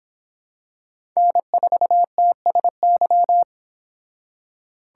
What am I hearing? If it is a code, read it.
Morse code, heard as a keyed tone, about 26 wpm, 700 Hz. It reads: N4TSY